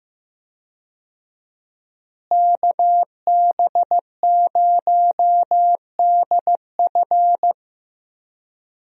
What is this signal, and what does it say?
Morse code, heard as a keyed tone, about 15 wpm, 700 Hz. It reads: KB0DF
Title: USA Call Signs - 5 Characters - Repeated 2x - 15wpm